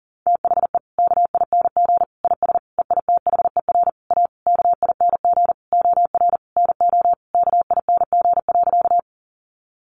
Morse code, heard as a keyed tone, about 40 wpm, 700 Hz. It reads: THE KING IS EITHER A KING OR NO KING.
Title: Sentences from Top 600 Words 40wpm